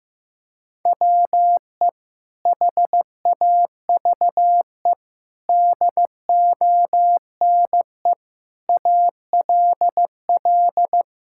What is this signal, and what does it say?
Morse code, heard as a keyed tone, about 14 wpm, 700 Hz. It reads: WE HAVE DONE ALL
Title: Sentences from Top 1000 Words - Review Part 5 - 15wpm